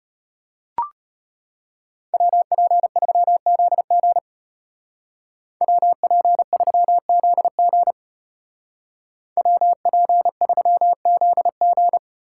Morse code, heard as a keyed tone, about 33 wpm, 700 Hz. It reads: WP3ZG WP3ZG WP3ZG